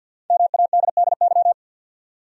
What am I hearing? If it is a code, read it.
Morse code, heard as a keyed tone, about 49 wpm, 700 Hz. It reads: MADDY
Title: USA Common Names - 50wpm